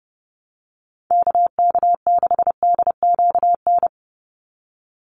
Morse code, heard as a keyed tone, about 30 wpm, 700 Hz. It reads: KK6DQN
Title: USA Call Signs 30wpm